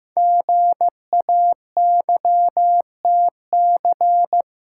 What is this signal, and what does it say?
Morse code, heard as a keyed tone, about 15 wpm, 700 Hz. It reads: GAYTC
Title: Code Groups - 5 Letters - 15wpm